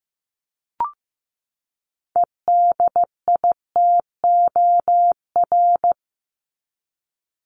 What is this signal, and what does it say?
Morse code, heard as a keyed tone, about 15 wpm, 700 Hz. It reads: EDITOR
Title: Top 1501-1750 Words 15wpm